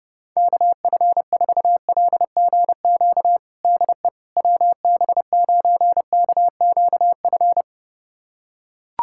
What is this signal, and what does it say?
Morse code, heard as a keyed tone, about 30 wpm, 700 Hz. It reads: KF4LGQ DE WB9KQF
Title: USA Callsign DE Callsign - 6 Characters - 30wpm